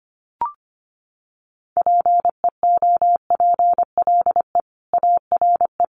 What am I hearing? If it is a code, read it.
Morse code, heard as a keyed tone, about 25 wpm, 700 Hz. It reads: PEOPLE ARE
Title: Sets of 4 Words 25wpm